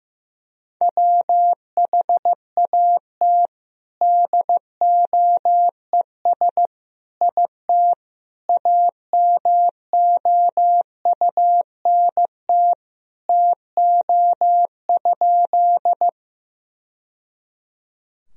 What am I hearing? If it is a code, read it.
Morse code, heard as a keyed tone, about 16 wpm, 700 Hz. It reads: WHAT DOES IT AMOUNT TO?
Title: Sentences from Top 1000 Words - Review Part 4 - 15wpm